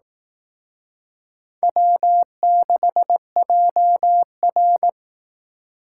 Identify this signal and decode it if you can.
Morse code, heard as a keyed tone, about 18 wpm, 700 Hz. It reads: W6JR
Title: Speed Racing - USA 1x2 Call Signs 15wpm